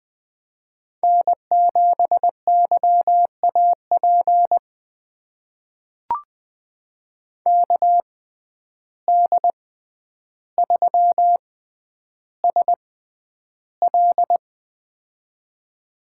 Morse code, heard as a keyed tone, about 20 wpm, spaced as for 10 wpm, 700 Hz. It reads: N7YAP KD3SL
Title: USA Call Signs - 5 Characters - 6x Character Spacing - 20wpm